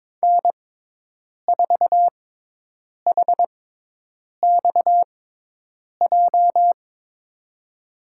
Morse code, heard as a keyed tone, about 22 wpm, spaced as for 8 wpm, 700 Hz. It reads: N4HXJ
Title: USA Call Signs - 5 Characters - 6x Character Spacing - 22wpm